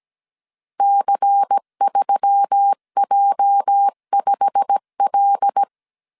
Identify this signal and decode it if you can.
Morse code, heard as a keyed tone, about 17 wpm, 800 Hz. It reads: C3J5L
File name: 1500C S B 16wpm